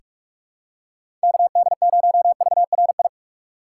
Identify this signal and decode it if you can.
Morse code, heard as a keyed tone, about 48 wpm, 700 Hz. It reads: KD0VRI
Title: USA Call Signs - 6 Characters - 6x Character Spacing - 45wpm